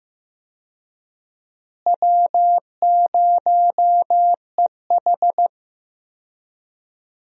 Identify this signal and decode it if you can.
Morse code, heard as a keyed tone, about 15 wpm, 700 Hz. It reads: W0EH
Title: USA Call Signs - 4 Characters - Repeated 3x - 15wpm